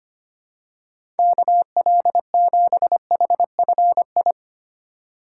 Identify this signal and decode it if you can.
Morse code, heard as a keyed tone, about 25 wpm, 700 Hz. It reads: KL7HFI
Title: K1USN SST QSO Practice - 25wpm